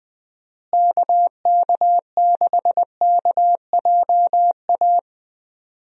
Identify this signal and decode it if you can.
Morse code, heard as a keyed tone, about 20 wpm, 700 Hz. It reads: KK6KJA
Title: USA Call Signs - 6 Characters - Repeated 3x - 20wpm